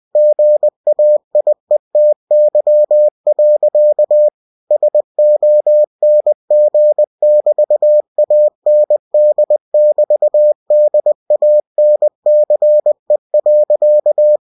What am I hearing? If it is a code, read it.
Morse code, heard as a keyed tone, about 20 wpm, 600 Hz. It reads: GAIETY. SONG<BT>AND<BT>DANCE.
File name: cw-books / winnie-the-pooh / winnie-20wpm-page-0045